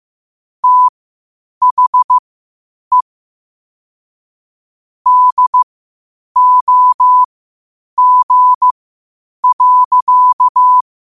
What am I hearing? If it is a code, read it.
Morse code, heard as a keyed tone, about 15 wpm, spaced as for 8 wpm, 1000 Hz. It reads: THE DOG.
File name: qotdfiles / 8wpm